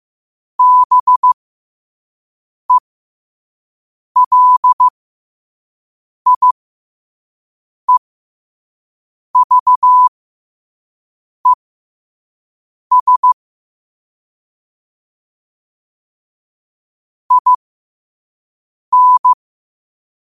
Morse code, heard as a keyed tone, about 15 wpm, spaced as for 5 wpm, 1000 Hz. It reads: BELIEVES IN
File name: qotdfiles / 5wpm